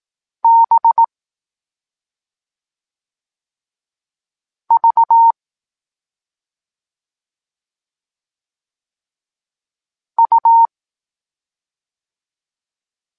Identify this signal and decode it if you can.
Morse code, heard as a keyed tone, about 18 wpm, 920 Hz. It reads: B V U
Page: Tekst 14 – doskonalenie odbioru poznanych liter